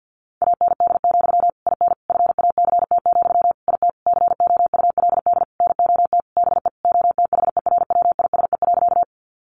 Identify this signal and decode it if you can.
Morse code, heard as a keyed tone, about 50 wpm, 700 Hz. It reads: AND, IN FACT, IT COULD NOT BE OTHERWISE.
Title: Sentences from Top 1200 Words 50wpm